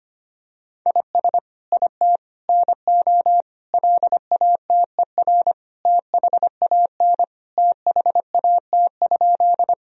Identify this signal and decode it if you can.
Morse code, heard as a keyed tone, about 25 wpm, 700 Hz. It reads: IS IT NO LATER THAN THAT?
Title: Sentences from Top 1000 Words - Review Part 4 - 25wpm